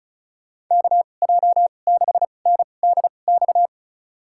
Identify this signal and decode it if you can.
Morse code, heard as a keyed tone, about 36 wpm, spaced as for 25 wpm, 700 Hz. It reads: KJ6NDX
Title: USA Call Signs - 6 Characters - 2x Character Spacing - 35wpm